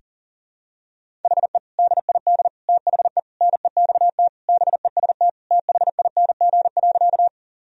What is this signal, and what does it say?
Morse code, heard as a keyed tone, about 40 wpm, 700 Hz. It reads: HE DID THE NEXT BEST THING.
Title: Sentences from Top 1000 Words - Review Part 1 - 40wpm